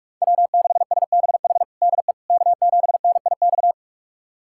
Morse code, heard as a keyed tone, about 46 wpm, 700 Hz. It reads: W6SBH DE K7NIX
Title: USA Callsign DE Callsign 45wpm